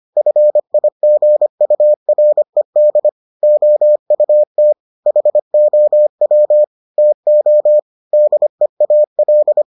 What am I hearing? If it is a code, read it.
Morse code, heard as a keyed tone, about 25 wpm, 600 Hz. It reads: FIGURED OUT HOW TO DEAL